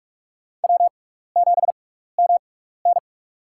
Morse code, heard as a keyed tone, about 47 wpm, spaced as for 16 wpm, 700 Hz. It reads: W7MN
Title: USA Call Signs - 6x Character Spacing - 45wpm